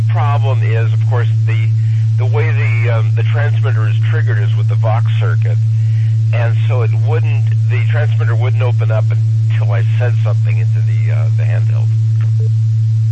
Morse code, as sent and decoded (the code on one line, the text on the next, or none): none